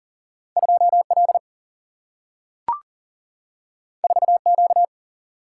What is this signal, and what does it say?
Morse code, heard as a keyed tone, about 39 wpm, 700 Hz. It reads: 2L 4Q
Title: Single-Letter Single-Number 40wpm